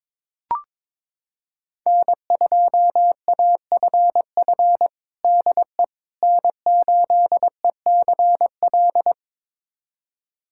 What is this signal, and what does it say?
Morse code, heard as a keyed tone, about 22 wpm, 700 Hz. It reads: N2AFF DE N8ECL
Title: USA Callsign DE Callsign - 5 Characters - 22wpm